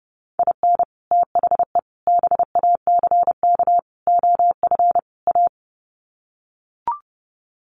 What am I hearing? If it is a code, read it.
Morse code, heard as a keyed tone, about 30 wpm, 700 Hz. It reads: IN THE BACK OF A
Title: Sets of 5 Words Encore2 30wpm